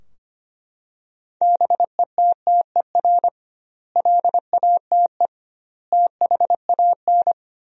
Morse code, heard as a keyed tone, about 25 wpm, 700 Hz. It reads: BETTER LATE THAN
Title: 2x Word-Spacing - Sentences from Top 400 Words - 25wpm